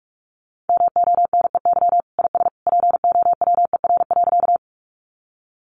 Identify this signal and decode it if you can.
Morse code, heard as a keyed tone, about 44 wpm, 700 Hz. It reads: MONEY IS POWER.
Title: Sentences from Top 500 Words 45wpm